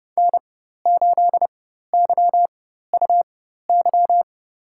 Morse code, heard as a keyed tone, about 30 wpm, spaced as for 14 wpm, 700 Hz. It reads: N8YUY
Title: USA Call Signs - 5 Characters - 4x Character Spacing - 30wpm